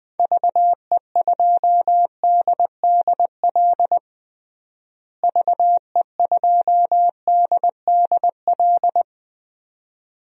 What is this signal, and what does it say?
Morse code, heard as a keyed tone, about 20 wpm, 700 Hz. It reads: VE2DDL VE2DDL
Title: DXCC-100 Call Signs - 6 Characters - Repeated 2x - 20wpm